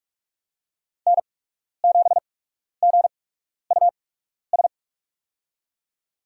Morse code, heard as a keyed tone, about 46 wpm, spaced as for 12 wpm, 700 Hz. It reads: N7GUS
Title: USA Call Signs - 5 Characters - 8x Character Spacing - 45wpm